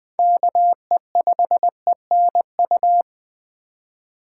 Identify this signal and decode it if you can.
Morse code, heard as a keyed tone, about 20 wpm, 700 Hz. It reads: KE5ENU
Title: USA Call Signs - 6 Characters - 4x Character Spacing - 20wpm